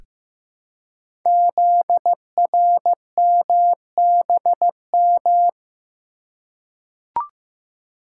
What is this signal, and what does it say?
Morse code, heard as a keyed tone, about 15 wpm, 690 Hz. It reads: ZRMBM
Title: Code Groups - 5 Letters - 15wpm